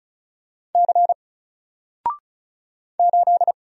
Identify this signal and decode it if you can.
Morse code, heard as a keyed tone, about 36 wpm, 700 Hz. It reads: C 8
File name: Single-Letter-Number-35wpm